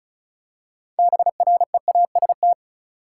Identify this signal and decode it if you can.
Morse code, heard as a keyed tone, about 34 wpm, 700 Hz. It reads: BREAST
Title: Six Letter Words - 35wpm